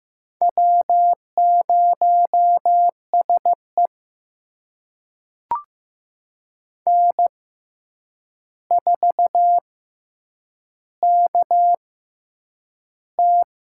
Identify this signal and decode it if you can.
Morse code, heard as a keyed tone, about 15 wpm, spaced as for 8 wpm, 700 Hz. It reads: W0SE N4KT
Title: USA Call Signs - 4 Characters - 6x Character Spacing - 15wpm